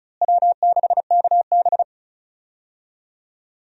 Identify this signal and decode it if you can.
Morse code, heard as a keyed tone, about 36 wpm, 700 Hz. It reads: W6KB
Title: USA Call Signs - 4 Characters - Repeated 3x - 35wpm